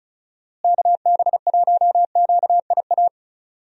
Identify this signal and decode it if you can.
Morse code, heard as a keyed tone, about 34 wpm, 700 Hz. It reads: KB1QIA
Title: USA Call Signs - 6 Characters - Repeated 3x - 35wpm